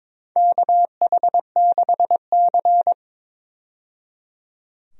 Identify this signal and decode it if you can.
Morse code, heard as a keyed tone, about 22 wpm, 700 Hz. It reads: KH6C
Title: DXCC-100 Call Signs - 4 Characters - 22wpm